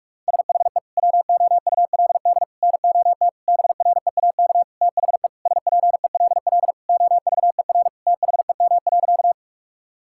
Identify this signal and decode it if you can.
Morse code, heard as a keyed tone, about 46 wpm, 700 Hz. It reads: SHE WOULD NOT BREAK THE SPELL OVER THEM.
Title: Sentences from Top 1000 Words - Review Part 6 - 45wpm